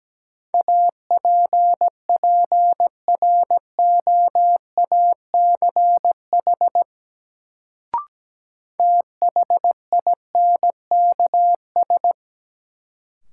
Morse code, heard as a keyed tone, about 17 wpm, 700 Hz. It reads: APPROACH THINKS
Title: Sentences from Top 1000 Words - Review Part 6 - New Word Drill - 17wpm